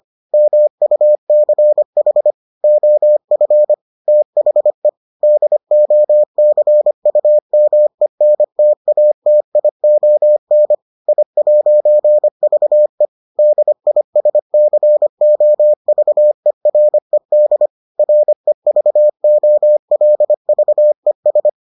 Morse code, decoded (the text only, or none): MUCH OF THE DOCUMENTATION I'VE DISCOVERED REVOLVES